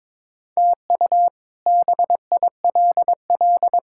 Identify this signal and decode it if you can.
Morse code, heard as a keyed tone, about 22 wpm, 700 Hz. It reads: TU BILL
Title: K1USN SST QSO Practice - 22wpm